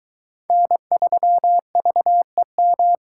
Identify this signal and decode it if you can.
Morse code, heard as a keyed tone, about 23 wpm, 700 Hz. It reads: N3VEM